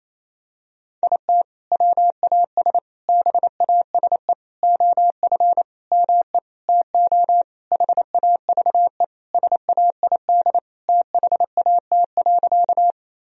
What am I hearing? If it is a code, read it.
Morse code, heard as a keyed tone, about 28 wpm, 700 Hz. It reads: IT WAS BASE OF ME TO HAVE SAID THAT.
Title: Sentences from Top 1000 Words - Review Part 1 - 28wpm